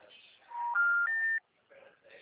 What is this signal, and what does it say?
Special information tone.
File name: out- -2035- - - .28